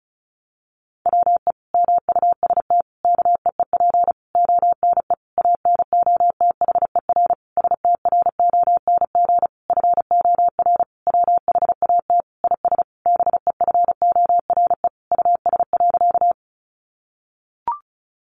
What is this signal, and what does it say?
Morse code, heard as a keyed tone, about 34 wpm, 700 Hz. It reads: WE MUST KEEP ONE ANOTHER STRONG FOR WHAT IS BEFORE US.
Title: Sentences from Top 600 Words 35wpm